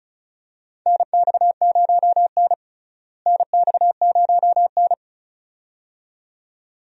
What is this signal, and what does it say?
Morse code, heard as a keyed tone, about 36 wpm, 700 Hz. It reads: NX0N NX0N
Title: USA Call Signs - 4 Characters - Repeated 2x - 35wpm